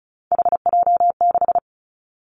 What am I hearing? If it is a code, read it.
Morse code, heard as a keyed tone, about 36 wpm, 700 Hz. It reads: HJ6
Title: Callsign Prefixes for Top 100 DXCC - 35wpm